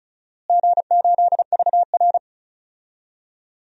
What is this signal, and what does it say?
Morse code, heard as a keyed tone, about 36 wpm, 700 Hz. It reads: G8VR
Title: DXCC-100 Call Signs - 4 Characters - Repeated 3x - 35wpm